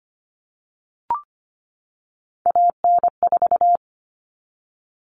Morse code, heard as a keyed tone, about 25 wpm, 700 Hz. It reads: AN4
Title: All Cut Numbers - 3-Digit 25wpm